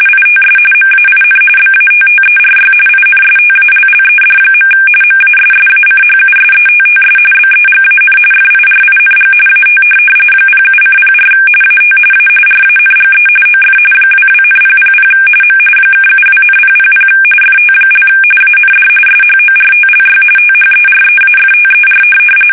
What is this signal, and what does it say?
RTTY, about 45.45 baud. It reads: ITS MAIN BATTERY INSTALLED. MASSACHUSETTS BEGAN FIRING AT 07:04 AT A RANGE OF 22,000 M (72,000 FT) AND SHORTLY THEREAFTER
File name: fsk850 / USS-Massachusetts